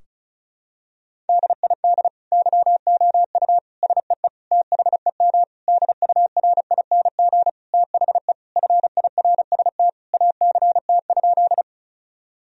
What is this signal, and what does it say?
Morse code, heard as a keyed tone, about 34 wpm, 700 Hz. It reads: DID YOU SEE THEM DURING THE FIRST ACT?
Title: Sentences from Top 1000 Words - Review Part 2 - 35wpm